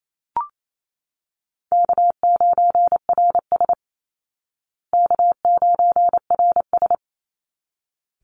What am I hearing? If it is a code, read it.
Morse code, heard as a keyed tone, about 28 wpm, 700 Hz. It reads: K9RS K9RS
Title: Easy Call Signs Repeated 2x - 28wpm